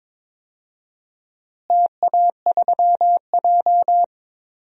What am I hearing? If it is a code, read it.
Morse code, heard as a keyed tone, about 22 wpm, 700 Hz. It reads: TA3J
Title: DXCC-100 Call Signs - 4 Characters - Repeated 3x - 22wpm